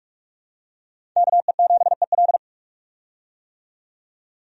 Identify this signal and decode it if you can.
Morse code, heard as a keyed tone, about 46 wpm, 700 Hz. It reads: KE7EL
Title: USA Call Signs - 5 Characters - 45wpm